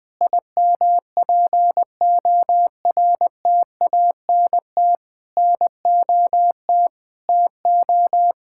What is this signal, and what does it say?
Morse code, heard as a keyed tone, about 20 wpm, 700 Hz. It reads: IMPORTANT NOT TO
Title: Sets of 9 Words - 20wpm